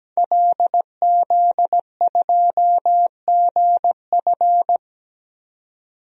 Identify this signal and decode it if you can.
Morse code, heard as a keyed tone, about 17 wpm, 700 Hz. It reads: LZ2GF
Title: DXCC-100 Call Signs - 5 Characters - 17wpm